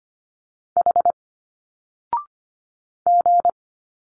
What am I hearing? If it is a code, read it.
Morse code, heard as a keyed tone, about 25 wpm, 700 Hz. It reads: H G
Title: Single Letters 25wpm